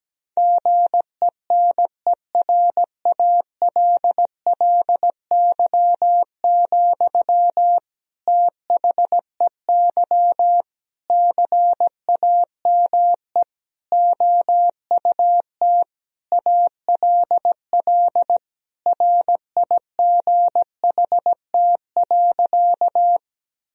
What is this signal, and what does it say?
Morse code, heard as a keyed tone, about 17 wpm, 700 Hz. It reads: GENERALLY, THEY CAME OUT ALL RIGHT.
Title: Sentences from Top 1000 Words - Review Part 4 - 17wpm